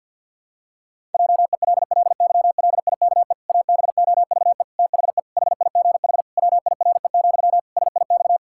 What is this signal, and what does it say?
Morse code, heard as a keyed tone, about 52 wpm, 700 Hz. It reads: JELLYLIKE ABOVE THE HIGH WIRE, SIX
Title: Pangram Challenge v2 - 50wpm